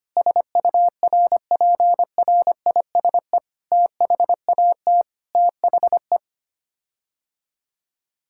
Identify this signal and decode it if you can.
Morse code, heard as a keyed tone, about 25 wpm, 700 Hz. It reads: SURPRISE THAT THE
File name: Sets of 9 Words - 25wpm